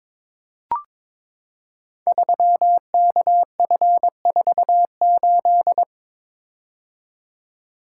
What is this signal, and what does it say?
Morse code, heard as a keyed tone, about 22 wpm, 700 Hz. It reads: 3KF48
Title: Code Groups - 5 Letter:Numbers - 22wpm